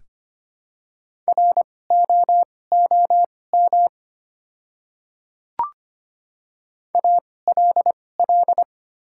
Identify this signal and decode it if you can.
Morse code, heard as a keyed tone, about 25 wpm, spaced as for 18 wpm, 690 Hz. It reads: ROOM ALL
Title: 2x Character Spacing - Top 300 Words - 25wpm